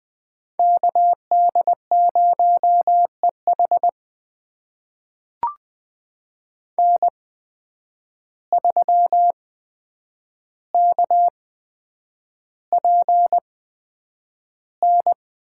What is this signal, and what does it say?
Morse code, heard as a keyed tone, about 20 wpm, spaced as for 9 wpm, 700 Hz. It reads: KD0EH N3KPN
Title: USA Call Signs - 5 Characters - 8x Character Spacing - 20wpm